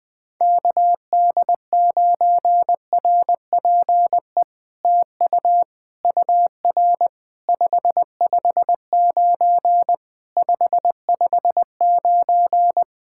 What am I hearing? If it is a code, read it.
Morse code, heard as a keyed tone, about 20 wpm, 700 Hz. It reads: KD9RPE TU UR 559 559
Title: POTA QSO Practice - 20wpm